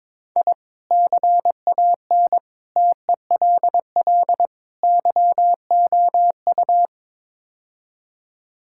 Morse code, heard as a keyed tone, about 22 wpm, 700 Hz. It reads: I CAN TELL YOU
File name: practice / 22wpm / Build A Phrase - 5-Word Phrases - 22wpm